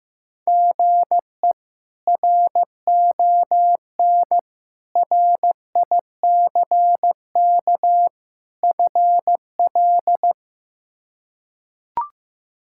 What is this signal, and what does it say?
Morse code, heard as a keyed tone, about 15 wpm, 700 Hz. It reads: GE RON RICK FL
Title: K1USN SST QSO Practice - 15wpm